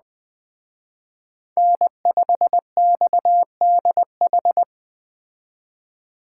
Morse code, heard as a keyed tone, about 20 wpm, 700 Hz. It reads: N5XDH
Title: Speed Racing - USA 1x3 Call Signs 17wpm